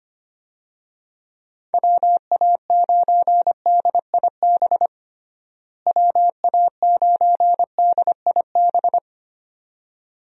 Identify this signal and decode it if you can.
Morse code, heard as a keyed tone, about 25 wpm, 700 Hz. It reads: WA9DIB WA9DIB
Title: USA Call Signs - 6 Characters - Repeated 2x - 25wpm